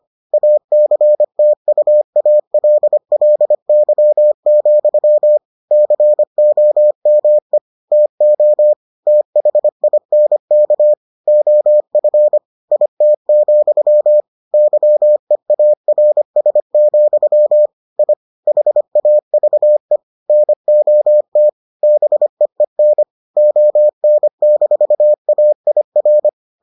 Morse code, heard as a keyed tone, about 25 wpm, 600 Hz. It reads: ACTUALLY, COME TO THINK OF IT, YEARS, I HAVE NOT BEEN ON-AIR